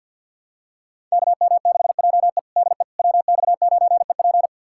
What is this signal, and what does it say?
Morse code, heard as a keyed tone, about 49 wpm, 700 Hz. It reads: KM6JE DE WX9EP
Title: USA Callsign DE Callsign 50wpm